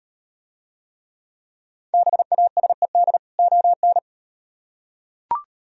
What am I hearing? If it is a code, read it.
Morse code, heard as a keyed tone, about 38 wpm, 700 Hz. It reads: BASED ON